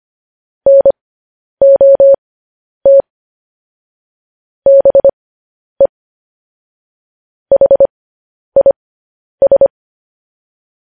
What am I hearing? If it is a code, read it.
Morse code, heard as a keyed tone, about 25 wpm, spaced as for 10 wpm, 550 Hz. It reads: NOT BE HIS
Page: news-headlines-in-morse-code-at-10-wpm